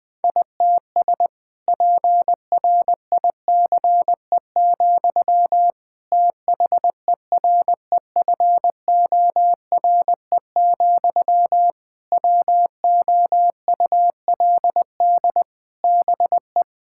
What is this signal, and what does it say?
Morse code, heard as a keyed tone, about 20 wpm, 700 Hz. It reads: ITS PRICE, THEREFORE, WOULD BE
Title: Sentences from Top 1000 Words - Review Part 5 - 20wpm